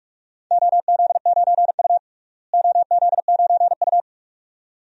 Morse code, heard as a keyed tone, about 42 wpm, 700 Hz. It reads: OZ9U OZ9U
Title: DXCC-100 Call Signs - 4 Characters - Repeated 2x - 45wpm